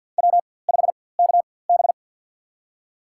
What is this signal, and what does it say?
Morse code, heard as a keyed tone, about 52 wpm, spaced as for 23 wpm, 700 Hz. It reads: W5KB
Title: USA Call Signs - 4x Character Spacing - 50wpm